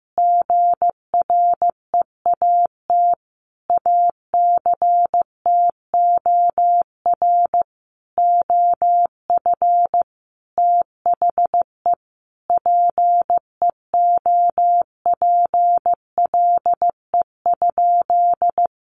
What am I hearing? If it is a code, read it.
Morse code, heard as a keyed tone, about 15 wpm, 700 Hz. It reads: GREAT ACTOR OF THE PEOPLE?
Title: Sentences from Top 1500 Words 15wpm